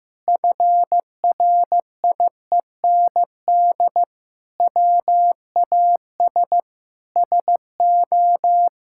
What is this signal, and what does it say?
Morse code, heard as a keyed tone, about 16 wpm, 700 Hz. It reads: FRIEND WAS SO
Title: Sentences from Top 1000 Words - Review Part 5 - 15wpm